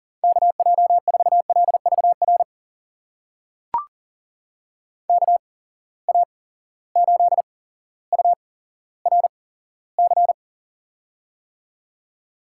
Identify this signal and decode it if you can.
Morse code, heard as a keyed tone, about 38 wpm, spaced as for 17 wpm, 700 Hz. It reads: KJ4LVR KA8URC
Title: USA Call Signs - 8x Character Spacing - 40wpm